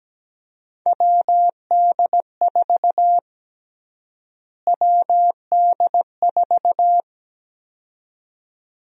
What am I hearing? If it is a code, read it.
Morse code, heard as a keyed tone, about 17 wpm, 700 Hz. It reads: WD4 WD4
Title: USA Call Sign Prefixes - Repeated 2x - 17wpm